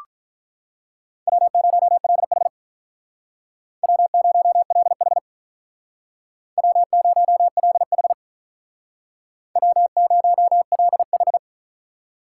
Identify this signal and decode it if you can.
Morse code, heard as a keyed tone, about 45 wpm, 700 Hz. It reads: W0LH W0LH W0LH W0LH